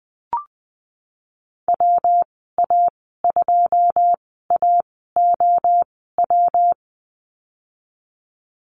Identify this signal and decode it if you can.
Morse code, heard as a keyed tone, about 20 wpm, spaced as for 14 wpm, 700 Hz. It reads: WA2AOW
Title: 147 Lesson 24g - Letter y - Call Signs 20wpm